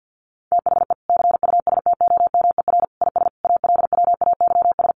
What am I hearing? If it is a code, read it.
Morse code, heard as a keyed tone, about 50 wpm, 700 Hz. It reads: THE CUSTOMER IS ALWAYS
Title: Sentences from Top 1100 Words 50wpm